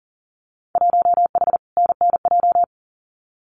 Morse code, heard as a keyed tone, about 40 wpm, 700 Hz. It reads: 1H NNJ
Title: Winter Field Day Exchanges 40wpm